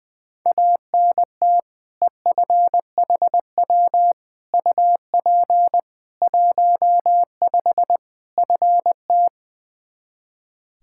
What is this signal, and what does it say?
Morse code, heard as a keyed tone, about 21 wpm, 700 Hz. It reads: ANT EFHW UP 15 FT
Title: Practice QSO - Exchange 2 - Info - 20wpm